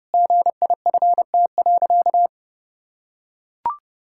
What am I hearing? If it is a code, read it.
Morse code, heard as a keyed tone, about 30 wpm, 700 Hz. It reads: GIFT.
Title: Sentences from Top 1000 Words - Review Part 6 - 30wpm